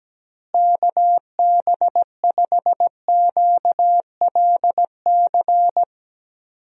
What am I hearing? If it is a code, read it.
Morse code, heard as a keyed tone, about 17 wpm, 700 Hz. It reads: KB5QLC
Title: USA Call Signs - 6 Characters - Repeated 2x - 17wpm